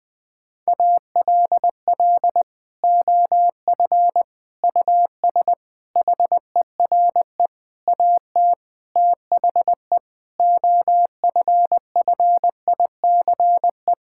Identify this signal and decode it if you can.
Morse code, heard as a keyed tone, about 20 wpm, 700 Hz. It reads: ALL OF US HERE AT THE OFFICE
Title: Sentences with 5-Word Phrases - 20wpm